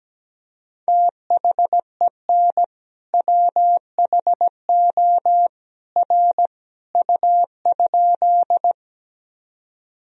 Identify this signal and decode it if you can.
Morse code, heard as a keyed tone, about 17 wpm, 700 Hz. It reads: THEN WHO R U?
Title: Sentences from Top 100 Words and Abbreviations 17wpm